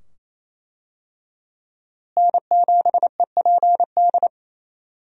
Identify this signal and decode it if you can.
Morse code, heard as a keyed tone, about 28 wpm, 700 Hz. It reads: N7EPD
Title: USA Call Signs - 5 Characters - Repeated 2x - 28wpm